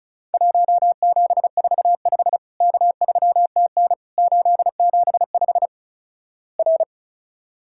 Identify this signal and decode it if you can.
Morse code, heard as a keyed tone, about 36 wpm, 700 Hz. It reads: 1745 K3TN 875 R
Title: Worked All Europe DX Contest - QTC Exchange - 35wpm